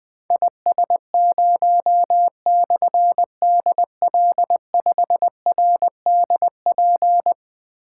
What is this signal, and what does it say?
Morse code, heard as a keyed tone, about 20 wpm, 700 Hz. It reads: IS0/DL5RDP
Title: Difficult Call Signs 20wpm